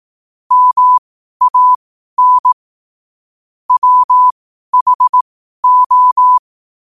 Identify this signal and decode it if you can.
Morse code, heard as a keyed tone, about 18 wpm, spaced as for 12 wpm, 1000 Hz. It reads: MAN WHO
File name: qotdfiles / 12wpm